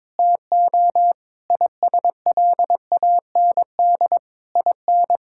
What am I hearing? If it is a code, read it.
Morse code, heard as a keyed tone, about 22 wpm, 700 Hz. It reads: TO ISLAND IN
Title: Sentences from Top 1000 Words - Review Part 6 - 22wpm